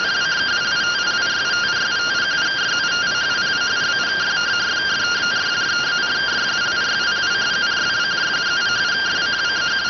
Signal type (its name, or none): two-tone teleprinter signal